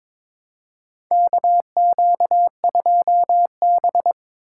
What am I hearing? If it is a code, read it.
Morse code, heard as a keyed tone, about 22 wpm, 700 Hz. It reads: KQ2B
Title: USA Call Signs - 4 Characters - 3x Character Spacing - 22wpm